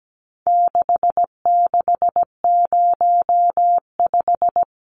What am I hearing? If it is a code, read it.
Morse code, heard as a keyed tone, about 17 wpm, 700 Hz. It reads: 6605
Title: Numbers - 4-Digit 17wpm